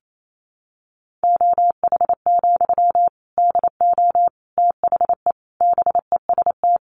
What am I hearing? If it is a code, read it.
Morse code, heard as a keyed tone, about 28 wpm, 700 Hz. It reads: OH, DO THE BEST
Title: Sentences from Top 500 Words 28wpm